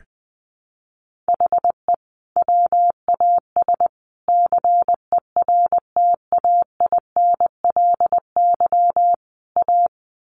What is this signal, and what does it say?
Morse code, heard as a keyed tone, about 20 wpm, 700 Hz. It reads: HE WAS CERTAINLY A
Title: Sentences from Top 1200 Words 20wpm